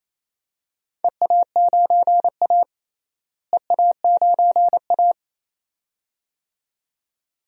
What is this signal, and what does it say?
Morse code, heard as a keyed tone, about 28 wpm, 700 Hz. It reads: EA9A EA9A
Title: DXCC-100 Call Signs - 4 Characters - Repeated 2x - 28wpm